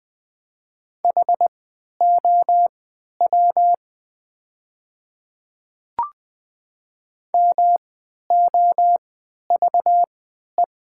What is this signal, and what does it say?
Morse code, heard as a keyed tone, about 20 wpm, spaced as for 11 wpm, 700 Hz. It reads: HOW MOVE
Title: 3x Character Spacing - Top 200 Words - 20wpm